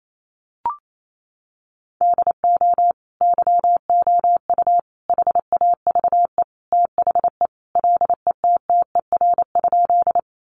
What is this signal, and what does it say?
Morse code, heard as a keyed tone, about 28 wpm, 700 Hz. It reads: DO YOU HAVE THE LETTER?
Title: Sentences from Top 300 Words 28wpm